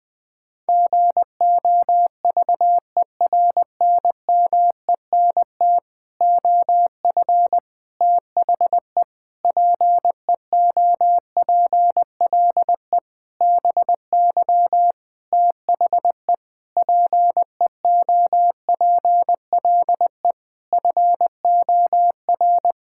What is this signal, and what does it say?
Morse code, heard as a keyed tone, about 20 wpm, 700 Hz. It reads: GOVERNMENT OF THE PEOPLE BY THE PEOPLE FOR
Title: Sentences with 9-Word Phrases - Spoken Beginning - 20wpm